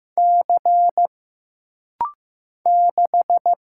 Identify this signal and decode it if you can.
Morse code, heard as a keyed tone, about 15 wpm, 700 Hz. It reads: C 6
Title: Single Letter-Number Alt - Rapid-Fire 15wpm